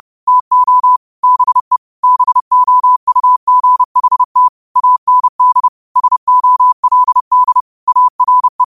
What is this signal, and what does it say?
Morse code, heard as a keyed tone, about 30 wpm, 1000 Hz. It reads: TO BE BOUGHT AND SOLD ARE
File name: qotdfiles / 30wpm